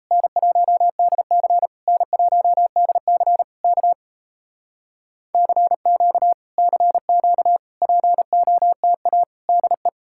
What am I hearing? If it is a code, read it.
Morse code, heard as a keyed tone, about 38 wpm, 700 Hz. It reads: N1DC N1DC K CQ CQ POTA DE